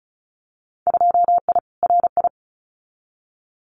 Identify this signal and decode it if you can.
Morse code, heard as a keyed tone, about 34 wpm, 700 Hz. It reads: 2I RI
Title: Winter Field Day Exchanges Encore 35wpm